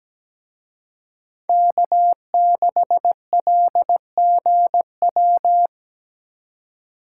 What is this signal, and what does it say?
Morse code, heard as a keyed tone, about 21 wpm, 700 Hz. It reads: K6LGW
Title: Speed Racing - USA Call Signs 17wpm